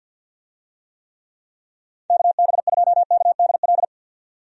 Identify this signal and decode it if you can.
Morse code, heard as a keyed tone, about 45 wpm, 700 Hz. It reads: KB2KDL
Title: USA Call Signs - 6 Characters - 50wpm